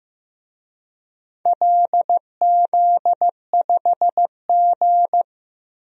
Morse code, heard as a keyed tone, about 15 wpm, 700 Hz. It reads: LZ5G
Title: DXCC-100 Call Signs - 4 Characters - 15wpm